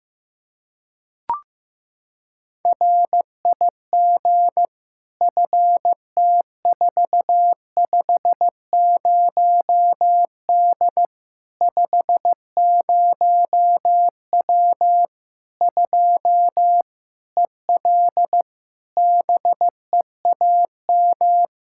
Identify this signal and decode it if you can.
Morse code, heard as a keyed tone, about 15 wpm, 700 Hz. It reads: RIG FT450D 50W 2 EL BEAM
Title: Practice QSO - Exchange 2 - Info - 15wpm